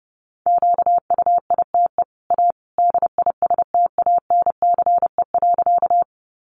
Morse code, heard as a keyed tone, about 30 wpm, 700 Hz. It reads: QUITE A DISTANCE.
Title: Sentences from Top 1200 Words 30wpm